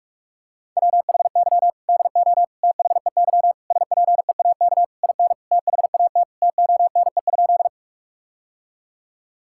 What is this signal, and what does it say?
Morse code, heard as a keyed tone, about 41 wpm, 700 Hz. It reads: WHY DO THEY SPEAK IN THAT TONE?
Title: Sentences from Top 1600 Words 45wpm